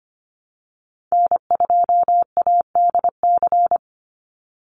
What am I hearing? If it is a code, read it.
Morse code, heard as a keyed tone, about 25 wpm, 700 Hz. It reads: N2ADC
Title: Call Signs Repeated 3x - 25wpm